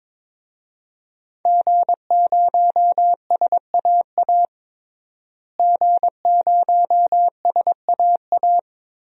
Morse code, heard as a keyed tone, about 22 wpm, 700 Hz. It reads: G0SAA G0SAA
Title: DXCC-100 Call Signs - 5 Characters - Repeated 2x - 22wpm